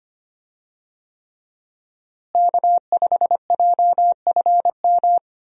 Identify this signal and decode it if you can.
Morse code, heard as a keyed tone, about 25 wpm, 700 Hz. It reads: K5JFM
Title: USA Call Signs - 5 Characters - Repeated 3x - 25wpm